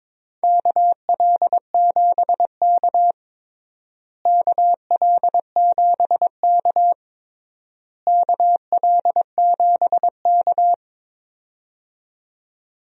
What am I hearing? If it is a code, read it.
Morse code, heard as a keyed tone, about 21 wpm, 700 Hz. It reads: KL7K KL7K KL7K
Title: DXCC-100 Call Signs - 4 Characters - Repeated 3x - 22wpm